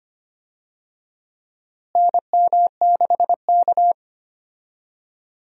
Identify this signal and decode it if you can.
Morse code, heard as a keyed tone, about 25 wpm, 700 Hz. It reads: NM6K
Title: USA Call Signs - 4 Characters - Repeated 2x - 25wpm